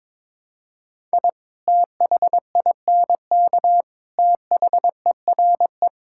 Morse code, heard as a keyed tone, about 22 wpm, 700 Hz. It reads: I THINK THERE
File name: practice / 22wpm / Build A Phrase - 5-Word Phrases - 22wpm